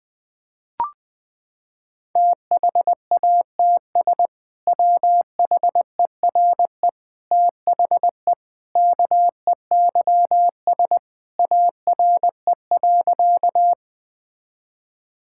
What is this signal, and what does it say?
Morse code, heard as a keyed tone, about 20 wpm, 700 Hz. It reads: THATS WHERE THE KEYS ARE.
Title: Sentences with 2-Word Phrases - 20wpm